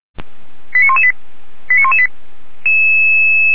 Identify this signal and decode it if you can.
Five-tone selective call (ZVEI), heard as quick runs of five tones, tones of about 70 ms. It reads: 89118 89118